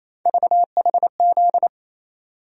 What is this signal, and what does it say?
Morse code, heard as a keyed tone, about 27 wpm, 700 Hz. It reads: VHZ
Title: USA Call Sign Postfixes - 3 Letters 28wpm